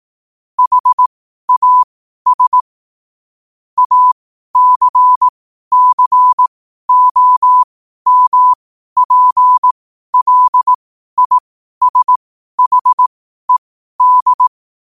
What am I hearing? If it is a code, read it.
Morse code, heard as a keyed tone, about 18 wpm, spaced as for 12 wpm, 1000 Hz. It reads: HAS ACCOMPLISHED
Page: Quotes for Fri, 15 Aug 2025 in Morse Code at 12 words per minute.